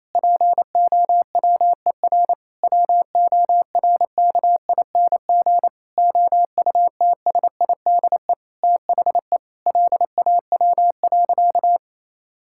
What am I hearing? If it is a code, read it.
Morse code, heard as a keyed tone, about 27 wpm, 700 Hz. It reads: POWER WORKING OUTSIDE THE LAW.
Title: Sentences from Top 1000 Words - Review Part 6 - 28wpm